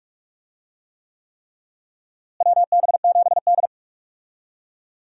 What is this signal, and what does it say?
Morse code, heard as a keyed tone, about 46 wpm, 700 Hz. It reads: WB7D
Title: USA Call Signs - 4 Characters - Repeated 3x - 45wpm